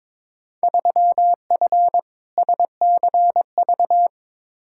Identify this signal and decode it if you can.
Morse code, heard as a keyed tone, about 22 wpm, 700 Hz. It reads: 3F SCV
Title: Field Day Exchanges - 22wpm